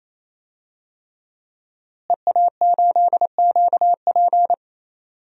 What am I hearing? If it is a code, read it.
Morse code, heard as a keyed tone, about 28 wpm, 700 Hz. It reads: EA8QP
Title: DXCC-100 Call Signs - 5 Characters - Repeated 3x - 28wpm